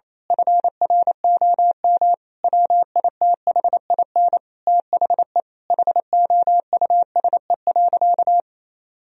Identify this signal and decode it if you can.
Morse code, heard as a keyed tone, about 28 wpm, 700 Hz. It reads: FROM WITHIN THE HOUSE.
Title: Sentences from Top 1000 Words - Review Part 4 - 28wpm